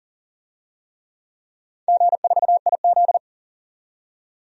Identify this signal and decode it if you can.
Morse code, heard as a keyed tone, about 40 wpm, 700 Hz. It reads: G4IZ
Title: DXCC-100 Call Signs - 4 Characters - 40wpm